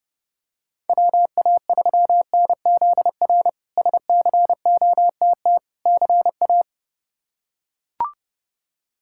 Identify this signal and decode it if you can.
Morse code, heard as a keyed tone, about 30 wpm, 700 Hz. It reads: WA3NZR SCOTT CA
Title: K1USN SST QSO Practice - 30wpm